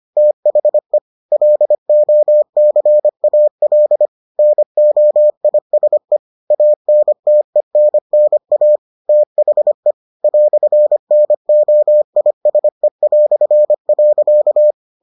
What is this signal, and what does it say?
Morse code, heard as a keyed tone, about 25 wpm, 600 Hz. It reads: THE LOCAL NOISE ANTENNA THE "NOISE".